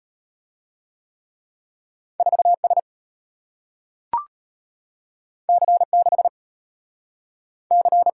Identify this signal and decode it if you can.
Morse code, heard as a keyed tone, about 33 wpm, 690 Hz. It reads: 4S C6 C